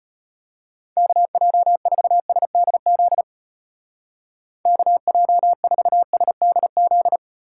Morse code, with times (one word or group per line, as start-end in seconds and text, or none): 1.0-3.2 KJ4SDZ
4.7-7.2 KJ4SDZ